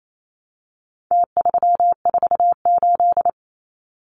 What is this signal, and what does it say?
Morse code, heard as a keyed tone, about 28 wpm, 700 Hz. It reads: T348
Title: Numbers - Contest Style - 4-Digit 28wpm